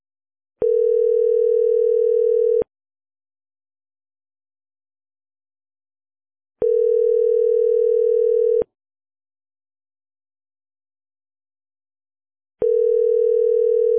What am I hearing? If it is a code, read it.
Ringback tone (North America), 440 + 480 Hz.